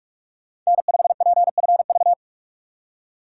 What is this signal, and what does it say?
Morse code, heard as a keyed tone, about 46 wpm, 700 Hz. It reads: N5PFV
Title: USA Call Signs - 4x Character Spacing - 45wpm